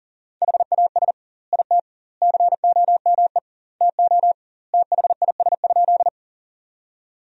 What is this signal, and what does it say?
Morse code, heard as a keyed tone, about 40 wpm, 700 Hz. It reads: HAS IT COME TO THIS?
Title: 2x Word-Spacing - Sentences from Top 100 Words - 40wpm